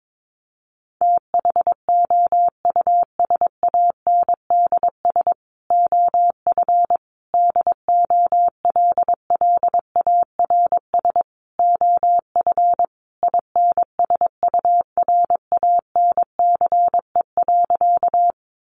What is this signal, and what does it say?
Morse code, heard as a keyed tone, about 22 wpm, 700 Hz. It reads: THOUSANDS OF DOLLARS OF INSURANCE.
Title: A Sound Of Thunder 22wpm